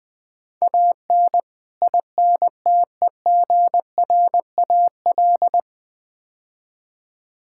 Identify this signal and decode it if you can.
Morse code, heard as a keyed tone, about 20 wpm, 700 Hz. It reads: AN INTEGRAL
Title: Build A Phrase - 5-Word Phrases - 20wpm